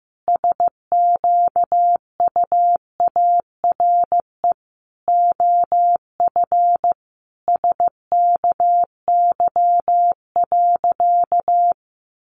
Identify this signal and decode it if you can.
Morse code, heard as a keyed tone, about 15 wpm, 700 Hz. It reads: SQUARE OF SKY.
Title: Sentences from Top 1200 Words 15wpm